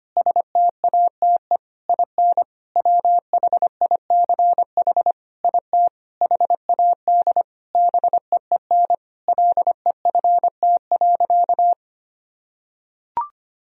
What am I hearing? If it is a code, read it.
Morse code, heard as a keyed tone, about 25 wpm, 700 Hz. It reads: STATE IN WHICH IT HAD BEEN LEFT.
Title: Sentences from Top 1000 Words - Review Part 2 - 25wpm